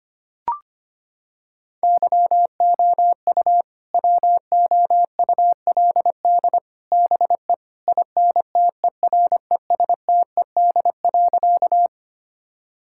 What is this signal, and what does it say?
Morse code, heard as a keyed tone, about 25 wpm, 700 Hz. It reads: YOU WOULD BE INTERESTED.
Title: Sentences from Top 1000 Words - Review Part 6 - 25wpm